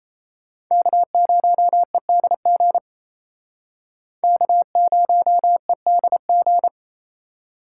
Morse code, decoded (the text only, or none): K0EDG K0EDG